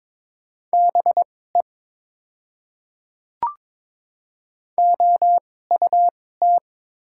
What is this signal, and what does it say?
Morse code, heard as a keyed tone, about 22 wpm, spaced as for 16 wpm, 700 Hz. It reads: BE OUT
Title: 2x Character Spacing - Top 100 Words - 22wpm